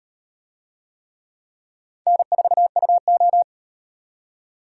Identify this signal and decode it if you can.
Morse code, heard as a keyed tone, about 36 wpm, 700 Hz. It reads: N4UO